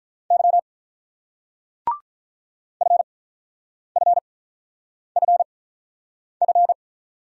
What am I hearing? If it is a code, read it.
Morse code, heard as a keyed tone, about 46 wpm, spaced as for 9 wpm, 700 Hz. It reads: Y FFFF